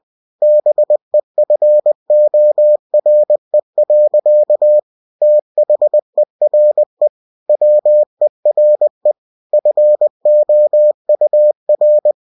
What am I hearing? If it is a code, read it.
Morse code, heard as a keyed tone, about 20 wpm, 600 Hz. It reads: BEFORE. THERE WERE FOUR
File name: cw-books / winnie-the-pooh / winnie-20wpm-page-0023